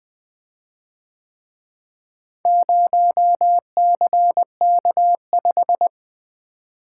Morse code, heard as a keyed tone, about 23 wpm, 700 Hz. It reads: 0CK5
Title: Speed-Racing - Four Letter-Number Combinations - 20wpm